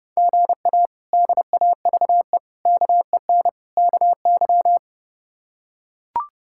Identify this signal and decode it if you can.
Morse code, heard as a keyed tone, about 32 wpm, 700 Hz. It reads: GA DAVE KEN KY
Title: K1USN SST QSO Practice - 30wpm